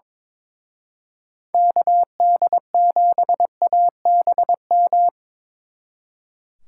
Morse code, heard as a keyed tone, about 25 wpm, 700 Hz. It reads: KD7ABM
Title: Speed Racing - USA 2x3 Call Signs 22wpm